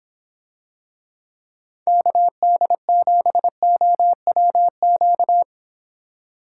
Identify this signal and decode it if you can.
Morse code, heard as a keyed tone, about 26 wpm, 700 Hz. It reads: KD7OWQ